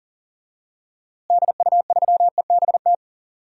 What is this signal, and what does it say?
Morse code, heard as a keyed tone, about 40 wpm, 700 Hz. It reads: DU3EBT
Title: DXCC-100 Call Signs - 6 Characters - Repeated 2x - 40wpm